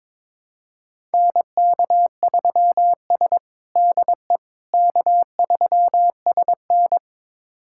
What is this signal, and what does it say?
Morse code, heard as a keyed tone, about 22 wpm, 700 Hz. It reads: NK3S DE K3SN
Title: USA Callsign DE Callsign - 4 Characters - 22wpm